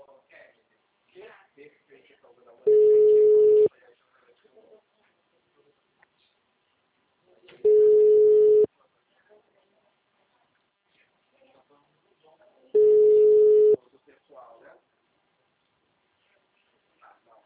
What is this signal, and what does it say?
Ringback tone (Europe, 425 Hz), with speech.
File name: out- -2002- - - .8225